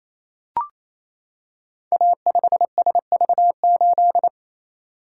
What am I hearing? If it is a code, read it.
Morse code, heard as a keyed tone, about 27 wpm, 700 Hz. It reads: A5SV8
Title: Code Groups - 5 Letter:Numbers - 28wpm